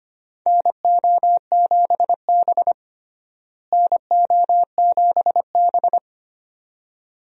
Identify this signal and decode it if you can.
Morse code, heard as a keyed tone, about 25 wpm, 700 Hz. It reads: NO7B NO7B
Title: USA Call Signs - 4 Characters - Repeated 2x - 25wpm